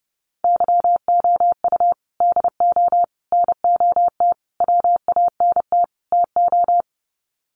Morse code, heard as a keyed tone, about 30 wpm, 700 Hz. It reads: YOU DO NOT WANT TO
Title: Sets of 5 Words 30wpm